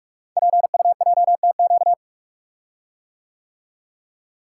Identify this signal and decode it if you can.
Morse code, heard as a keyed tone, about 44 wpm, 700 Hz. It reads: PUJTQ
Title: Code Groups - 5 Letter:Numbers - 45wpm